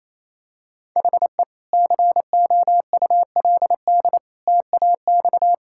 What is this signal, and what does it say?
Morse code, heard as a keyed tone, about 28 wpm, 700 Hz. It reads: HE COULD TAX
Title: Sentences from Top 1000 Words - Review Part 5 - 28wpm